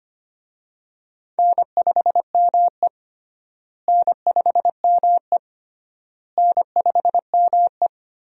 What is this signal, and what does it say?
Morse code, heard as a keyed tone, about 25 wpm, 700 Hz. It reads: N5ME N5ME N5ME
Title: USA Call Signs - 4 Characters - Repeated 3x - 25wpm